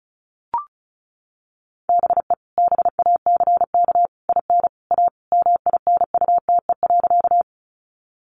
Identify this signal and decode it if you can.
Morse code, heard as a keyed tone, about 34 wpm, 700 Hz. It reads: BE BACK IN A MINUTE.
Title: Sentences from Top 500 Words 35wpm